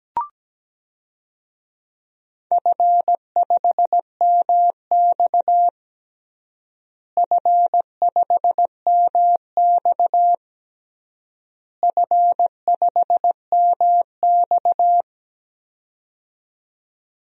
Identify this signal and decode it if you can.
Morse code, heard as a keyed tone, about 17 wpm, 700 Hz. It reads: F5MX F5MX F5MX
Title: DXCC-100 Call Signs - 4 Characters - Repeated 3x - 17wpm